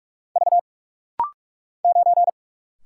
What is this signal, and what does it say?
Morse code, heard as a keyed tone, about 44 wpm, 700 Hz. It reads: V 9
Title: Single Letter-Number - Rapid-Fire 45wpm